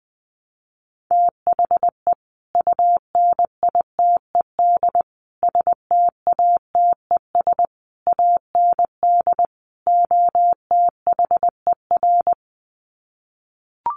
Morse code, heard as a keyed tone, about 20 wpm, 700 Hz. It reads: THE UNITED STATES AND OTHER
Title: Sets of 5 Words Encore 20wpm